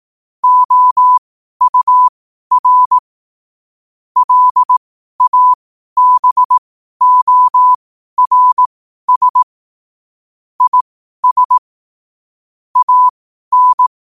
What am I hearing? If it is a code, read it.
Morse code, heard as a keyed tone, about 18 wpm, spaced as for 12 wpm, 1000 Hz. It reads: OUR LABORS IS AN
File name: qotdfiles / 12wpm